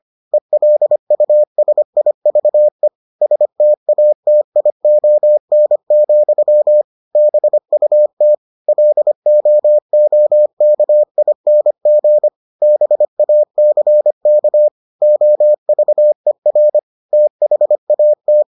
Morse code, heard as a keyed tone, about 25 wpm, 600 Hz. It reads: ELUSIVE STATION, BUT LOOKING BACK OVER THAT